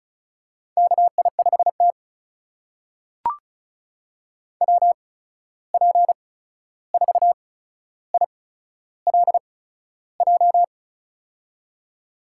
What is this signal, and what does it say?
Morse code, heard as a keyed tone, about 36 wpm, spaced as for 13 wpm, 700 Hz. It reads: KI5T WP4ILJ
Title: USA Call Signs - 8x Character Spacing - 35wpm